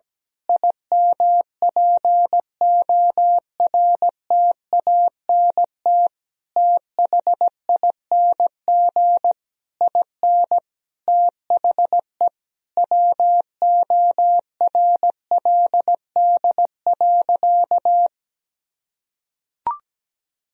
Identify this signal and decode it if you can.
Morse code, heard as a keyed tone, about 17 wpm, 700 Hz. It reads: IMPORTANT THING IN THE WORLD.
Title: Sentences from Top 1000 Words - Review Part 3 - 17wpm